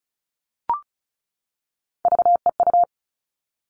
Morse code, heard as a keyed tone, about 34 wpm, 700 Hz. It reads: VEU
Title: All Cut Numbers - 3-Digit 35wpm